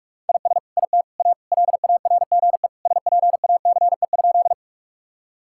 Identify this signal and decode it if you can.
Morse code, heard as a keyed tone, about 41 wpm, 700 Hz. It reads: IS IT A LARGE SPACE?
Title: Sentences from Top 1000 Words - Review Part 5 - 45wpm